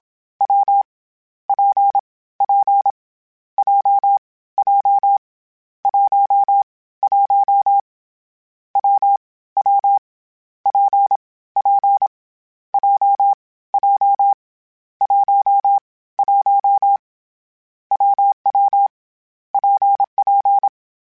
Morse code, heard as a keyed tone, about 27 wpm, 800 Hz. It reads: W P P J J 1 1 W W P P J J 1 1 WW PP
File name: Variable Speed / Variable Speed-mp3s / WPJ WPJ1 sp @25